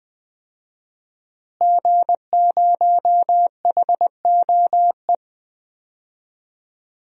Morse code, heard as a keyed tone, about 20 wpm, 700 Hz. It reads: G0HOE
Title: DXCC-100 Call Signs - 5 Characters - Repeated 3x - 20wpm